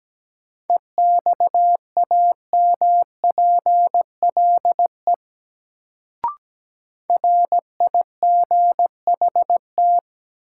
Morse code, heard as a keyed tone, about 17 wpm, 700 Hz. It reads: EXAMPLE RIGHT
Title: Sets of 2 Words - New Word Drill - 17wpm